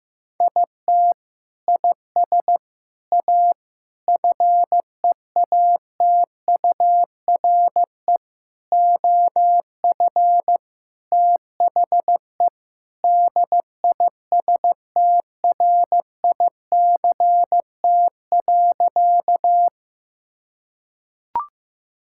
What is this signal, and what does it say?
Morse code, heard as a keyed tone, about 15 wpm, 700 Hz. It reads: IT IS A FEATURE OF THE DISTRICT.
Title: Sentences from Top 1600 Words 15wpm